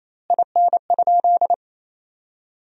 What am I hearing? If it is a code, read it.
Morse code, heard as a keyed tone, about 28 wpm, 700 Hz. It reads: IN?
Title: 2x Word-Spacing - Sentences from Top 300 Words - 28wpm